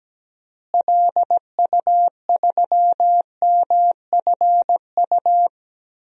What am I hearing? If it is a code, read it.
Morse code, heard as a keyed tone, about 17 wpm, 700 Hz. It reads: LU3MFU
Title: DXCC-100 Call Signs - 6 Characters - Repeated 2x - 17wpm